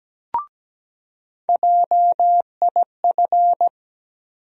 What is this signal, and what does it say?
Morse code, heard as a keyed tone, about 17 wpm, 700 Hz. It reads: JIF
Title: USA Call Sign Postfixes - 3 Letters 17wpm